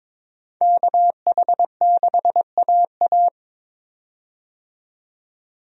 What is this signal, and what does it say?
Morse code, heard as a keyed tone, about 22 wpm, 700 Hz. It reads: KH6AA
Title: DXCC-100 Call Signs - 5 Characters - 22wpm